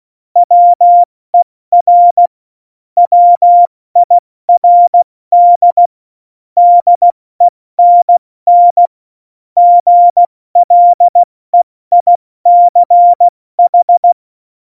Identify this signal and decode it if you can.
Morse code, heard as a keyed tone, about 16 wpm, 700 Hz. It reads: WER WIRD DENN GLEICH
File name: Morsekurs / Witze / Witz 17.Dezember 2019, 70 F80